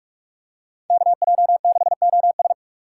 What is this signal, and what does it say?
Morse code, heard as a keyed tone, about 42 wpm, 700 Hz. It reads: KJ6OS
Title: USA Call Signs - 5 Characters - 3x Character Spacing - 45wpm